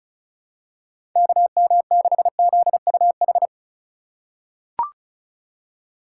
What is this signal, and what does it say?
Morse code, heard as a keyed tone, about 36 wpm, 700 Hz. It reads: KM6ZUH
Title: USA Call Signs - 6 Characters - 4x Character Spacing - 35wpm